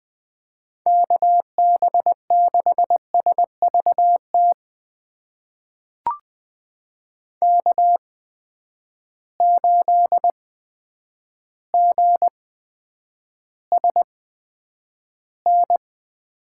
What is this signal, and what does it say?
Morse code, heard as a keyed tone, about 20 wpm, spaced as for 9 wpm, 700 Hz. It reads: KB6SVT K8GSN
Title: USA Call Signs - 8x Character Spacing - 20wpm